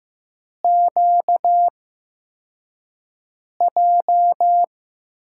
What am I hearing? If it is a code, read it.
Morse code, heard as a keyed tone, about 15 wpm, 700 Hz. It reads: Q J
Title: USA Call Signs - 5 Characters - 8x Character Spacing - 15wpm